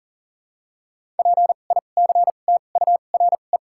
Morse code, heard as a keyed tone, about 40 wpm, spaced as for 29 wpm, 700 Hz. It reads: PICTURE
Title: 2x Character Spacing - Top 300 Words - 40wpm